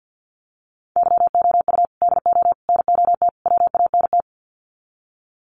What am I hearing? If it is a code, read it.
Morse code, heard as a keyed tone, about 50 wpm, 700 Hz. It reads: YOU DO NOT WANT
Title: Sets of 4 Words 50wpm